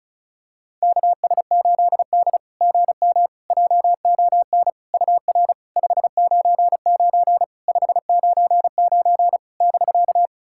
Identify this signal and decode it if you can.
Morse code, heard as a keyed tone, about 36 wpm, 690 Hz. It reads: KS8D GM JON UR 599 599 <BK>
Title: SOTA QSO Practice 35wpm